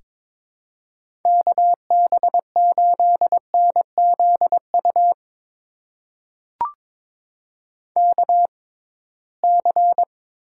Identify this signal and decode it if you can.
Morse code, heard as a keyed tone, about 22 wpm, spaced as for 17 wpm, 700 Hz. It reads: KB8NZU KC
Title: USA Call Signs - 6 Characters - 6x Character Spacing - 22wpm